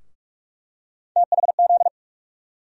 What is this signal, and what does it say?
Morse code, heard as a keyed tone, about 46 wpm, 700 Hz. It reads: THZ
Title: USA Call Sign Postfixes - 3 Letters 45wpm